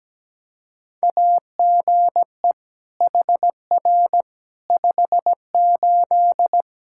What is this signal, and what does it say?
Morse code, heard as a keyed tone, about 17 wpm, 700 Hz. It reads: AGE HR 58
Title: Practice QSO - Exchange 3 - More Info - 17wpm